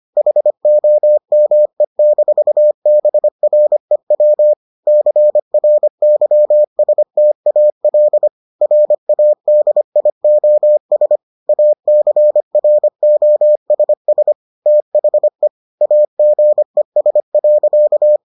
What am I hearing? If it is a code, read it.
Morse code, heard as a keyed tone, about 25 wpm, 600 Hz. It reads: HOME-BREW CRYSTAL RADIOS ACROSS THE AGES.